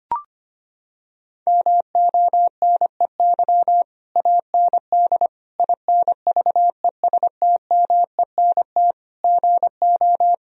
Morse code, heard as a keyed tone, about 25 wpm, 700 Hz. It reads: MONEY AND INVESTMENT GO
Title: Sentences from Top 1000 Words - Review Part 4 - 25wpm